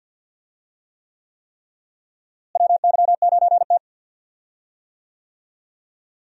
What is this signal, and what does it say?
Morse code, heard as a keyed tone, about 52 wpm, 700 Hz. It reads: WY9T
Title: USA Call Signs - 4 Characters - Repeated 3x - 50wpm